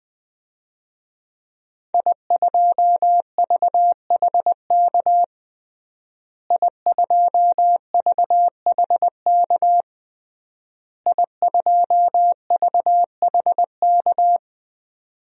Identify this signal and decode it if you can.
Morse code, heard as a keyed tone, about 20 wpm, 700 Hz. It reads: I2VHK I2VHK I2VHK
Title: DXCC-100 Call Signs - 5 Characters - Repeated 3x - 20wpm